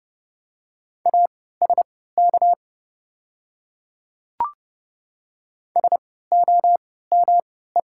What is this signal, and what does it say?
Morse code, heard as a keyed tone, about 31 wpm, spaced as for 17 wpm, 700 Hz. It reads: ASK SOME
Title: 3x Character Spacing - Top 200 Words - 30wpm